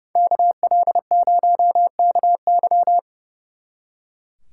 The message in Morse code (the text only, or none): KL0KY